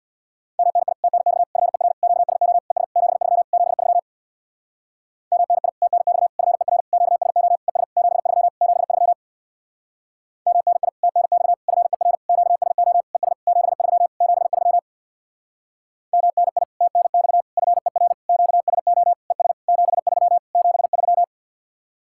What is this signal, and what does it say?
Morse code, heard as a keyed tone, about 66 wpm, 700 Hz. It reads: MNI TNX FER QSO ES 73 73 MNI TNX FER QSO ES 73 73 MNI TNX FER QSO ES 73 73 MNI TNX FER QSO ES 73 73